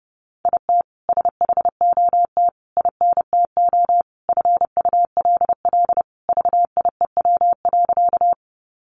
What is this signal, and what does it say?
Morse code, heard as a keyed tone, about 30 wpm, 700 Hz. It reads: IT SHOT INTO FULL VIEW.
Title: Sentences from Top 1100 Words 30wpm